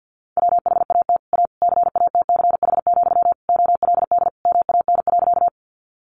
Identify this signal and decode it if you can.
Morse code, heard as a keyed tone, about 50 wpm, 700 Hz. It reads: WHAT A CATCH, OLD MAN.
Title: Sentences from Top 1500 Words 50wpm